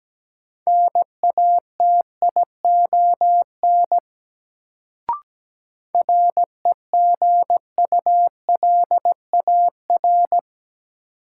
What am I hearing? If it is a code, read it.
Morse code, heard as a keyed tone, about 17 wpm, 700 Hz. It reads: NATION REGULAR
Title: Sentences from Top 900 Words - New Word Drill - 17wpm